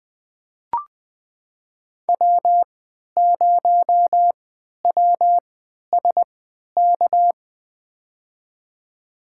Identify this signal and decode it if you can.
Morse code, heard as a keyed tone, about 20 wpm, spaced as for 11 wpm, 700 Hz. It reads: W0WSK
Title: USA Call Signs - 3x Character Spacing - 20wpm